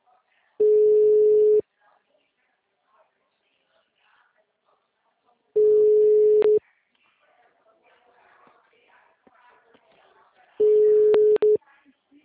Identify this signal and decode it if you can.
Ringback tone (Europe, 425 Hz), with speech.